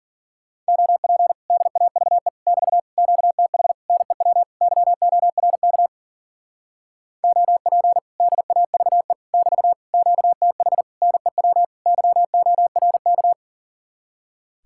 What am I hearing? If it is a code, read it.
Morse code, heard as a keyed tone, about 48 wpm, 700 Hz. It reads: OP DAVE <BT> QTH NEW YORK OP DAVE <BT> QTH NEW YORK
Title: Speed-Racing Common QSO Phrases Encore - 40wpm